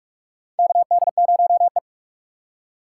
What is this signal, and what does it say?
Morse code, heard as a keyed tone, about 44 wpm, 700 Hz. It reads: KD0E
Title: USA Call Signs - 3x Character Spacing - 45wpm